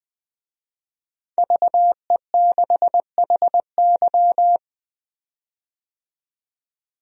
Morse code, heard as a keyed tone, about 21 wpm, 700 Hz. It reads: VE6HY
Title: DXCC-100 Call Signs - 5 Characters - 20wpm